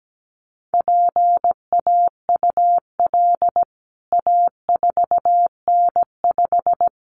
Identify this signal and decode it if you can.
Morse code, heard as a keyed tone, about 17 wpm, 700 Hz. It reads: PAUL A4N5
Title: CWT Exchange 17wpm